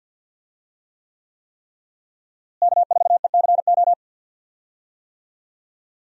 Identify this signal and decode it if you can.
Morse code, heard as a keyed tone, about 52 wpm, 700 Hz. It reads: K4ECO
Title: USA Call Signs - 5 Characters - Repeated 3x - 50wpm